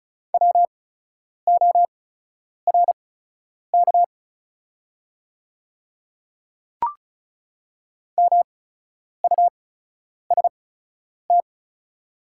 Morse code, heard as a keyed tone, about 36 wpm, spaced as for 10 wpm, 700 Hz. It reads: WORK MUST
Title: 8x Character Spacing - Top 200 Words - 35wpm